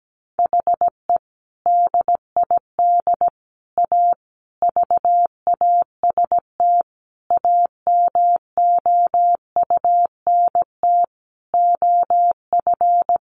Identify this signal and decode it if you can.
Morse code, heard as a keyed tone, about 17 wpm, 700 Hz. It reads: HE DID A VAST AMOUNT OF
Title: Sentences from Top 1300 Words 17wpm